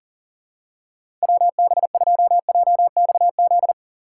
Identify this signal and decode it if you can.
Morse code, heard as a keyed tone, about 40 wpm, 700 Hz. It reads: WB2JXZ
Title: USA Call Signs - 6 Characters - 2x Character Spacing - 40wpm